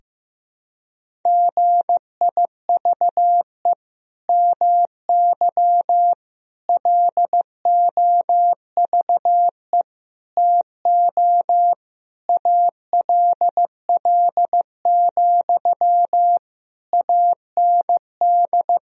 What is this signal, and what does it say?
Morse code, heard as a keyed tone, about 15 wpm, 700 Hz. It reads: GIVE MY LOVE TO ALL, AND
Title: Sentences from Top 1000 Words - Review Part 4 - 15wpm